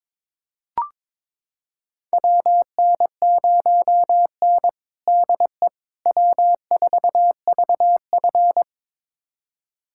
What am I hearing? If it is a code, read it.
Morse code, heard as a keyed tone, about 22 wpm, 700 Hz. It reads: WN0N DE W4VF
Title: USA Callsign DE Callsign - 4 Characters - 22wpm